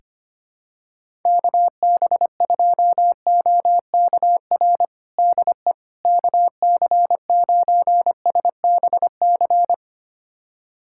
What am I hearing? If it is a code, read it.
Morse code, heard as a keyed tone, about 25 wpm, 700 Hz. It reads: KB2OKR DE KC9SBC
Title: USA Callsign DE Callsign - 6 Characters - 25wpm